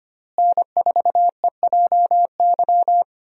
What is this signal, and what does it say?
Morse code, heard as a keyed tone, about 25 wpm, 700 Hz. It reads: N4EJY
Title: USA Call Signs - 5 Characters - Repeated 3x - 25wpm